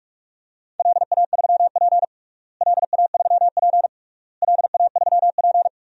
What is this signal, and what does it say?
Morse code, heard as a keyed tone, about 41 wpm, 700 Hz. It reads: LA3P LA3P LA3P
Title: DXCC-100 Call Signs - 4 Characters - Repeated 3x - 45wpm